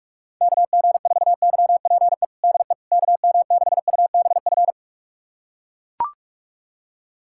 Morse code, heard as a keyed tone, about 42 wpm, 700 Hz. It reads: KG4YPE DE KM6UBF
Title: USA Callsign DE Callsign 45wpm